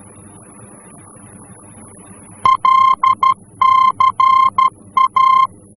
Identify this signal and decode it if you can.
Morse code, heard as a keyed tone, about 12 wpm, 1050 Hz. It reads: LCA
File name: france / mx / LCA-F-VOR